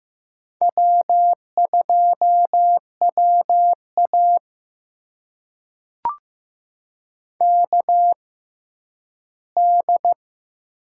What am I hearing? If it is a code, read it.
Morse code, heard as a keyed tone, about 15 wpm, spaced as for 10 wpm, 700 Hz. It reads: W2WA KD
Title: USA Call Signs - 6x Character Spacing - 15wpm